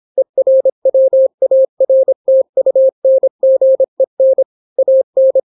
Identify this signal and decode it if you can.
Morse code, heard as a keyed tone, about 26 wpm, 530 Hz. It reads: ERWARTUNGEN AN